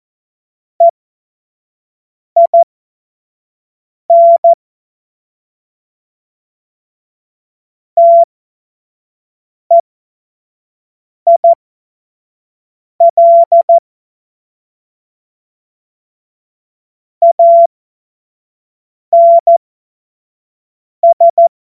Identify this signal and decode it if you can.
Morse code, heard as a keyed tone, about 14 wpm, spaced as for 5 wpm, 680 Hz. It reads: EIN TEIL ANS
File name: Morsekurs / Witze / Witz 03 Juni 2025 25 F70